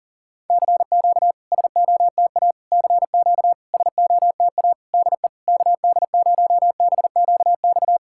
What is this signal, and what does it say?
Morse code, heard as a keyed tone, about 40 wpm, 700 Hz. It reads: CQ SOTA CQ SOTA DE KD0BQX
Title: SOTA QSO Practice 40wpm